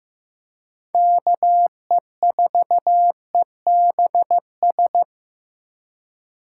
Morse code, heard as a keyed tone, about 18 wpm, 700 Hz. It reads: KE4EBS
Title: Speed Racing - USA Call Signs 15wpm